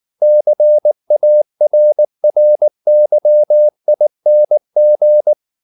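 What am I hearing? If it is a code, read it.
Morse code, heard as a keyed tone, about 19 wpm, 600 Hz. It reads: CARRYING
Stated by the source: Morse code at 19 wpm from CW med Gnister, reading Ken Follett's novel Never in English.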